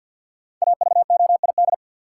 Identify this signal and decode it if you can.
Morse code, heard as a keyed tone, about 47 wpm, 700 Hz. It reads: AVOID
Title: Sentences from Top 1000 Words - Review Part 6 - New Word Drill - 50wpm